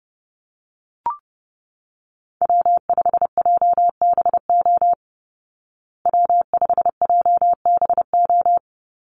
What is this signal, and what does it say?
Morse code, heard as a keyed tone, about 30 wpm, 700 Hz. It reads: W5JBO W5JBO
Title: Call Signs Repeated 2x - 30wpm